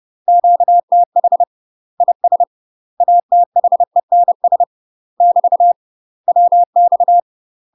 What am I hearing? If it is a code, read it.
Morse code, heard as a keyed tone, about 30 wpm, 700 Hz. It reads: QTH IS ATHENS <BT> WX
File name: UK MASTER QSOs TRAINING/30WPM